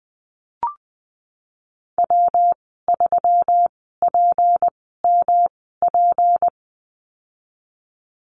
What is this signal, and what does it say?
Morse code, heard as a keyed tone, about 20 wpm, spaced as for 15 wpm, 700 Hz. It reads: W3PMP
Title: 167 Lesson 27g - Number 7 - Call Signs 20wpm